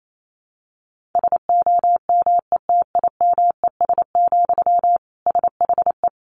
Morse code, heard as a keyed tone, about 28 wpm, 700 Hz. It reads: SOMETIMES, SHE
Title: Sentences from Top 1000 Words 28wpm